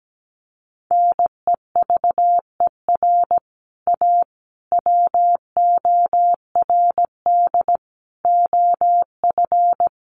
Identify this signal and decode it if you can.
Morse code, heard as a keyed tone, about 17 wpm, 700 Hz. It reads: NEVER A WORD OF
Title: Sentences from Top 1300 Words 17wpm